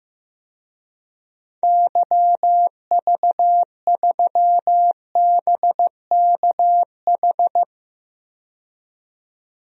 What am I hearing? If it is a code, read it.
Morse code, heard as a keyed tone, about 15 wpm, 700 Hz. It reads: YV3BKH
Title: DXCC-100 Call Signs - 6 Characters - 15wpm